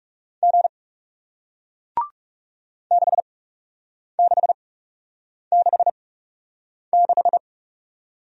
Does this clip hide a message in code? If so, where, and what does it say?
Yes: Morse code from 0.4 to 7.4 s, heard as a keyed tone, about 40 wpm, spaced as for 8 wpm, 700 Hz. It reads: G 6666